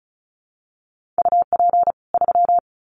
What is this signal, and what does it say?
Morse code, heard as a keyed tone, about 34 wpm, 700 Hz. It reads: UP 3
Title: Common QSO Phrases 35wpm